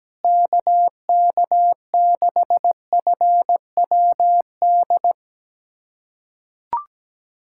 Morse code, heard as a keyed tone, about 17 wpm, 700 Hz. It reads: KK6FWD
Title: USA Callsign DE Callsign 17wpm